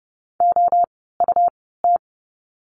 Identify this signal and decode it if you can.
Morse code, heard as a keyed tone, about 30 wpm, spaced as for 17 wpm, 700 Hz. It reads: OUT
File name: course / 30:10wpm / 230 Lesson 37f - Letter x - Words 30wpm